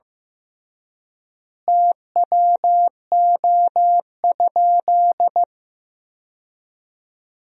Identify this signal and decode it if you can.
Morse code, heard as a keyed tone, about 15 wpm, 700 Hz. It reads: TWO?
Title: 3x Word-Spacing - Sentences from Top 100 Words - 15wpm